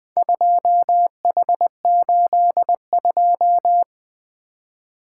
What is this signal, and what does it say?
Morse code, heard as a keyed tone, about 20 wpm, 700 Hz. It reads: 2H82
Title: Four Letter-Number Combinations - 20wpm